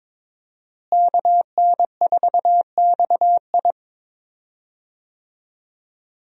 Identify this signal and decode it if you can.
Morse code, heard as a keyed tone, about 22 wpm, 700 Hz. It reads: KN4XI
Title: USA Call Signs - 5 Characters - Repeated 3x - 22wpm